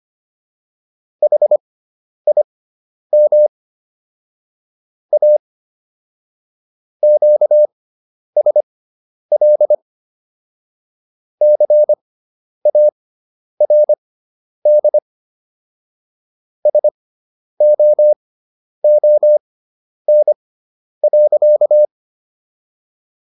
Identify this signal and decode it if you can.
Morse code, heard as a keyed tone, about 25 wpm, spaced as for 10 wpm, 600 Hz. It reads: HIM A QSL CARD SOON.